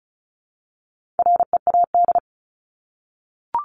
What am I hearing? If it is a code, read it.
Morse code, heard as a keyed tone, about 34 wpm, 700 Hz. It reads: READ
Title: Top 300 Words Repeated 35wpm